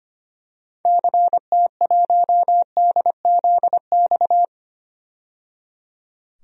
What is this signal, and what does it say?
Morse code, heard as a keyed tone, about 25 wpm, 700 Hz. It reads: CT1DZX
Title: DXCC-100 Call Signs - 6 Characters - 25wpm